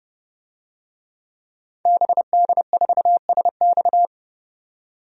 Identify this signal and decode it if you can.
Morse code, heard as a keyed tone, about 30 wpm, 700 Hz. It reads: BD4SX
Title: DXCC-100 Call Signs - 5 Characters - 30wpm